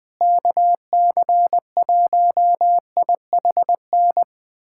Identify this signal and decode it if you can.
Morse code, heard as a keyed tone, about 20 wpm, 700 Hz. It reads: KC1IHN
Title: USA Callsign DE Callsign 20wpm